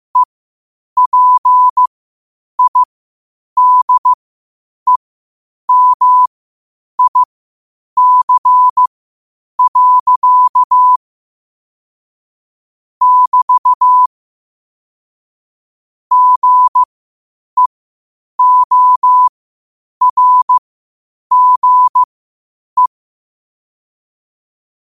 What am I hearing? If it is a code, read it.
Morse code, heard as a keyed tone, about 15 wpm, spaced as for 8 wpm, 1000 Hz. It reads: EPIDEMIC. <BT> GEORGE
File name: qotdfiles / 8wpm